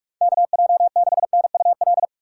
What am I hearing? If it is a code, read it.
Morse code, heard as a keyed tone, about 46 wpm, 700 Hz. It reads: KJ6NUL
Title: USA Call Signs - 6 Characters - 3x Character Spacing - 45wpm